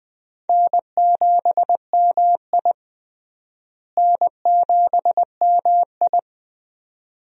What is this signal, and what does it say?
Morse code, heard as a keyed tone, about 21 wpm, 700 Hz. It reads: N7MI N7MI
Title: USA Call Signs - 4 Characters - Repeated 2x - 20wpm